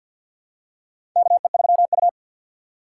Morse code, heard as a keyed tone, about 50 wpm, 700 Hz. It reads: KE3U
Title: USA Call Signs - 4 Characters - 6x Character Spacing - 50wpm